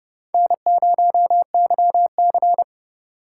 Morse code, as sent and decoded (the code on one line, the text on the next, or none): -. ----- -.-- -.-.
N0YC